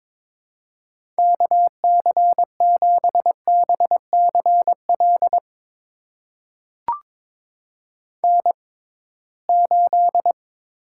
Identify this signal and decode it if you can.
Morse code, heard as a keyed tone, about 22 wpm, spaced as for 17 wpm, 700 Hz. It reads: KC7BCL N8
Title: USA Call Signs - 6x Character Spacing - 22wpm